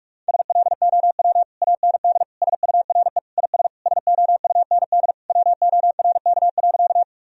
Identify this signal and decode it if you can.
Morse code, heard as a keyed tone, about 41 wpm, 700 Hz. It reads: SLOW AND SURE IS SOUND WORK.
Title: Sentences from Top 1000 Words - Review Part 2 - 45wpm